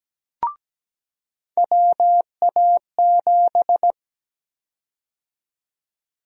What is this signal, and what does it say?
Morse code, heard as a keyed tone, about 17 wpm, 700 Hz. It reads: WA7
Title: USA Call Sign Prefixes - 17wpm